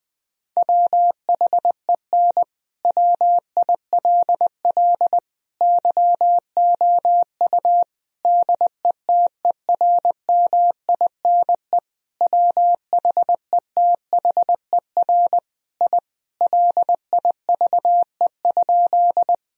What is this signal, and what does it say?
Morse code, heard as a keyed tone, about 20 wpm, 700 Hz. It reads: WHEN WILL YOU DETERMINE WHETHER I LIVE?
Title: Sentences from Top 1000 Words - Review Part 4 - 20wpm